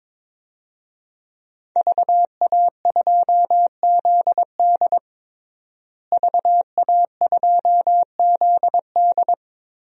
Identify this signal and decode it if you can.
Morse code, heard as a keyed tone, about 22 wpm, 700 Hz. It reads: VA2ZD VA2ZD
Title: DXCC-100 Call Signs - 5 Characters - Repeated 3x - 22wpm